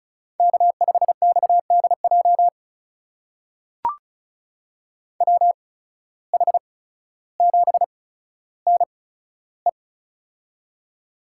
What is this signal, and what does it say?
Morse code, heard as a keyed tone, about 36 wpm, spaced as for 15 wpm, 700 Hz. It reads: K5XDJ WH7NE
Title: USA Call Signs - 5 Characters - 8x Character Spacing - 35wpm